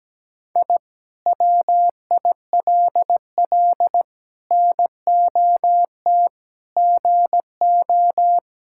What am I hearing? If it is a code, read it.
Morse code, heard as a keyed tone, about 17 wpm, 700 Hz. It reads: I WILL NOT GO
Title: Sentences from Top 1000 Words - Review Part 6 - 17wpm